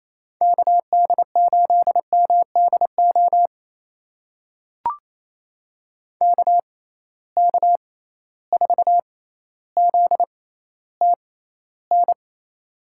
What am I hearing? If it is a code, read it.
Morse code, heard as a keyed tone, about 28 wpm, spaced as for 14 wpm, 700 Hz. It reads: KD8MDO KK4ZTN
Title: USA Call Signs - 6 Characters - 6x Character Spacing - 28wpm